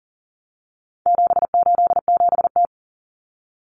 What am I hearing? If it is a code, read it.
Morse code, heard as a keyed tone, about 40 wpm, 700 Hz. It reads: 787T
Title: Common Cut Numbers - 4-Digit 40wpm